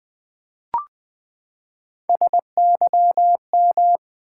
Morse code, heard as a keyed tone, about 20 wpm, 700 Hz. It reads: SYM
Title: Build A Word - 3 Letters At A Time - 8-Letter Words - 20wpm